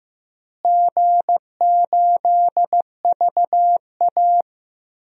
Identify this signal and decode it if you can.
Morse code, heard as a keyed tone, about 15 wpm, 700 Hz. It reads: G8VA
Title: DXCC-100 Call Signs - 4 Characters - Repeated 3x - 15wpm